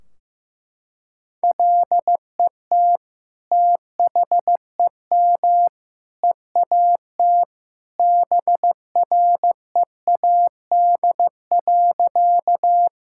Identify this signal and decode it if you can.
Morse code, heard as a keyed tone, about 16 wpm, 700 Hz. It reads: LET THEM EAT BREAD.
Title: Sentences from Top 1600 Words 15wpm